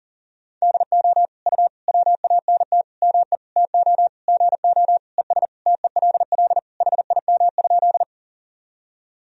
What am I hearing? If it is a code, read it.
Morse code, heard as a keyed tone, about 37 wpm, 700 Hz. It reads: DO U WANT ME TO GO ES TELL HIM?
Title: Sentences from Top 200 Words and Abbreviations 40wpm